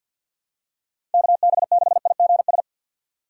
Morse code, heard as a keyed tone, about 53 wpm, 700 Hz. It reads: KB6IGS
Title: USA Call Signs - 6x Character Spacing - 50wpm